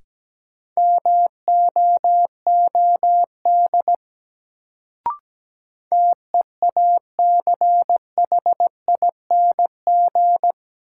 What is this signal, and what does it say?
Morse code, heard as a keyed tone, about 17 wpm, 700 Hz. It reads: MOOD TEACHING
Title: Sentences from Top 1100 Words - New Word Drill - 17wpm